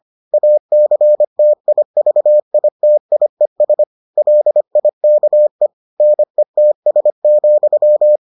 Morse code, heard as a keyed tone, about 25 wpm, 600 Hz. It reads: ACTIVITIES LIKE NETS,